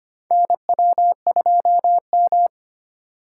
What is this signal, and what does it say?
Morse code, heard as a keyed tone, about 25 wpm, 700 Hz. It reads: NW2M
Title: USA Call Signs - 4x Character Spacing - 25wpm